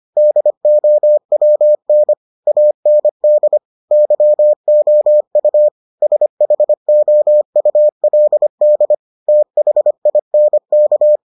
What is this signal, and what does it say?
Morse code, heard as a keyed tone, about 25 wpm, 600 Hz. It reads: DOWN AND YOU SHOULD THINK